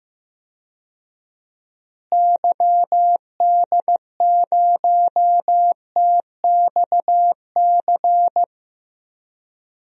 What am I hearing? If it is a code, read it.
Morse code, heard as a keyed tone, about 15 wpm, 700 Hz. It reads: YD0TXC
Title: DXCC-100 Call Signs - 6 Characters - Repeated 3x - 15wpm